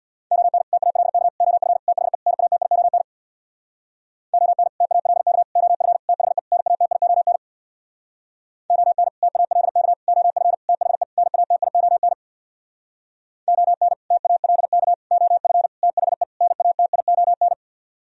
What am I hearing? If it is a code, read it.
Morse code, heard as a keyed tone, about 65 wpm, 700 Hz. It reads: ON TALK OF THE NATION ON TALK OF THE NATION ON TALK OF THE NATION ON TALK OF THE NATION